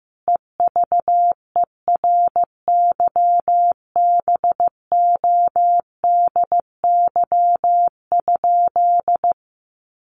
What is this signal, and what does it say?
Morse code, heard as a keyed tone, about 15 wpm, 700 Hz. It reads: EVERYBODY?
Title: Sentences from Top 1400 Words 15wpm